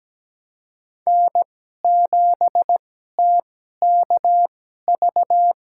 Morse code, heard as a keyed tone, about 17 wpm, spaced as for 12 wpm, 700 Hz. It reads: N7TKV
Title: USA Call Signs - 5 Characters - 2x Character Spacing - 17wpm